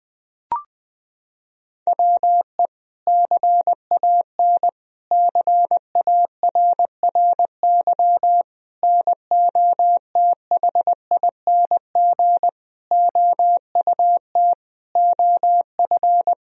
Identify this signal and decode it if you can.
Morse code, heard as a keyed tone, about 20 wpm, 700 Hz. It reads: WE CAN CARRY NOTHING OUT OF
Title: Sentences from Top 1000 Words - Review Part 1 - 20wpm